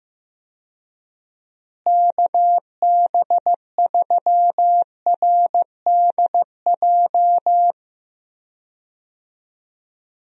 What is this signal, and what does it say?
Morse code, heard as a keyed tone, about 15 wpm, 700 Hz. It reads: KB3RDJ
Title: USA Call Signs - 6 Characters - Repeated 2x - 15wpm